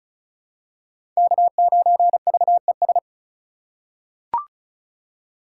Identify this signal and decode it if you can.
Morse code, heard as a keyed tone, about 34 wpm, 700 Hz. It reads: K9VES
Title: USA Call Signs - 5 Characters - 3x Character Spacing - 35wpm